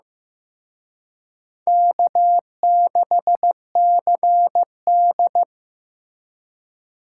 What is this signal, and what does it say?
Morse code, heard as a keyed tone, about 18 wpm, 700 Hz. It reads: K6CD
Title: Speed Racing - USA 1x2 Call Signs 15wpm